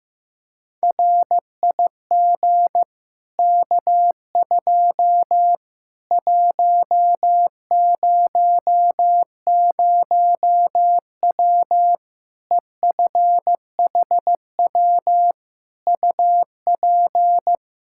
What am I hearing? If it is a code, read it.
Morse code, heard as a keyed tone, about 15 wpm, 700 Hz. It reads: RIG K2 100W EFHW UP
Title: Practice QSO - Exchange 2 - Info - 15wpm